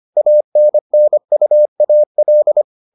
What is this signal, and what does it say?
Morse code, heard as a keyed tone, about 25 wpm, 600 Hz. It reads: ANNUAL